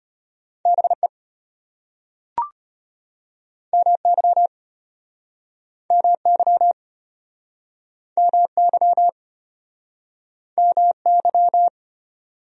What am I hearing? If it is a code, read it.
Morse code, heard as a keyed tone, about 31 wpm, 700 Hz. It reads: BE MY MY MY MY
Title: Speed-Racing Top 300 Words 25wpm